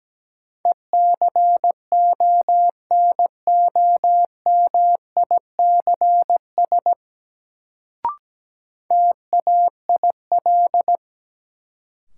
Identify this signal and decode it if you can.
Morse code, heard as a keyed tone, about 17 wpm, 700 Hz. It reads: ECONOMICS TAIL
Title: Sentences from Top 1000 Words - Review Part 3 - New Word Drill - 17wpm